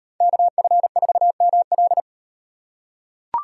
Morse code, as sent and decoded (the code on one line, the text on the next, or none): -.- ..-. ....- -- .-..
KF4ML